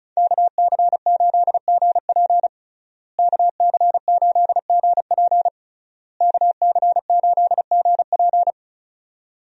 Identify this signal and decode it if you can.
Morse code, heard as a keyed tone, about 36 wpm, 700 Hz. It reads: KC8GP KC8GP KC8GP
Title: USA Call Signs - 5 Characters - Repeated 3x - 35wpm